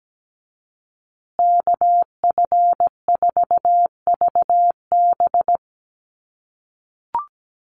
Morse code, heard as a keyed tone, about 17 wpm, 700 Hz. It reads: KF4VB
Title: USA Call Signs Repeated 2x 17wpm